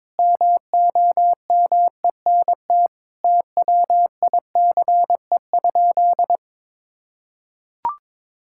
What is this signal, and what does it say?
Morse code, heard as a keyed tone, about 21 wpm, 700 Hz. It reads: MOMENT TWICE?
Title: Sentences from Top 1600 Words 22wpm